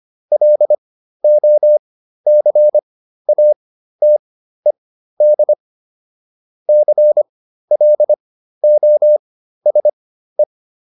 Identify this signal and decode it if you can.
Morse code, heard as a keyed tone, about 25 wpm, spaced as for 13 wpm, 600 Hz. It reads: LOCATED CLOSE